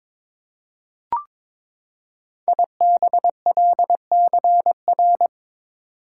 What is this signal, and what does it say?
Morse code, heard as a keyed tone, about 22 wpm, 700 Hz. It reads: IBLCR
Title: Code Groups - 5 Letter:Numbers - 22wpm